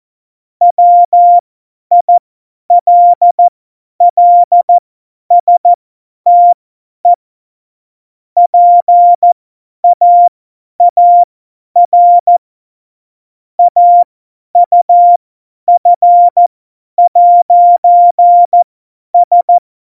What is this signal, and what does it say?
Morse code, heard as a keyed tone, about 14 wpm, spaced as for 10 wpm, 700 Hz. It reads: WILLSTE PAAR AUF'S
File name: Morsekurs / Witze / Witz 03. November 2020, 50 F70